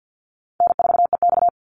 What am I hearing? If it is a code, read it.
Morse code, heard as a keyed tone, about 50 wpm, 700 Hz. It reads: N4EX
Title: NA SOTA Chaser Callsigns 50wpm